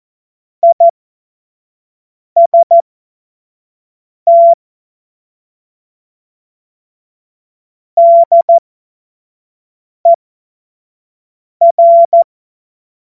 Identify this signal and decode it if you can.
Morse code, heard as a keyed tone, about 14 wpm, spaced as for 5 wpm, 680 Hz. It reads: IST DER
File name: Morsekurs / Witze / Witz 06. Januar 2026 25 F70